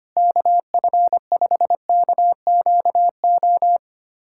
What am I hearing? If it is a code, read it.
Morse code, heard as a keyed tone, about 25 wpm, 700 Hz. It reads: KF5KQO
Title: USA Callsign DE Callsign - 6 Characters - 25wpm